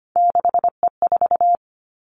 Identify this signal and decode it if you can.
Morse code, heard as a keyed tone, about 25 wpm, 700 Hz. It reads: 6E4
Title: Three Challenging Letter-Number Combinations 25wpm